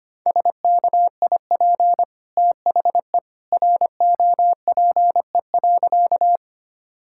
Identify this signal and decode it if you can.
Morse code, heard as a keyed tone, about 25 wpm, 700 Hz. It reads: SKIP THE ROPE.
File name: Easy Sentences with Hints - 25wpm